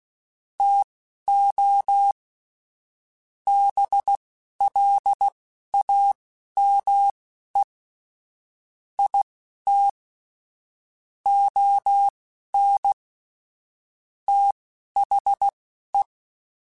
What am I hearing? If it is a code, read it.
Morse code, heard as a keyed tone, about 16 wpm, spaced as for 12 wpm, 800 Hz. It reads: TO BLAME IT ON THE
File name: Chuckles / Chuckles-mp3s / HumanErr @10